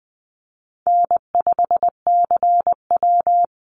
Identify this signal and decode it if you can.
Morse code, heard as a keyed tone, about 20 wpm, 700 Hz. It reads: N5CW
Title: Easy Call Signs - 20wpm